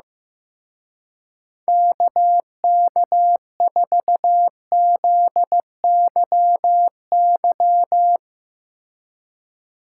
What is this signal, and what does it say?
Morse code, heard as a keyed tone, about 16 wpm, 700 Hz. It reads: KK4ZYY
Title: Speed Racing - USA 2x3 Call Signs 15wpm